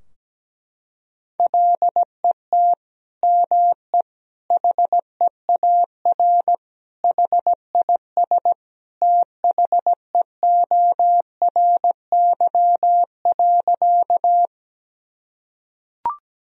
Morse code, heard as a keyed tone, about 17 wpm, 700 Hz. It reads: LET ME HEAR HIS THEORY.
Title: Sentences from Top 1000 Words - Review Part 3 - 17wpm